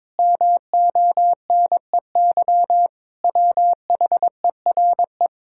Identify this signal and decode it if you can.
Morse code, heard as a keyed tone, about 22 wpm, 700 Hz. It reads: MONEY WHERE
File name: practice / 22wpm / Sets of 8 Words - 22wpm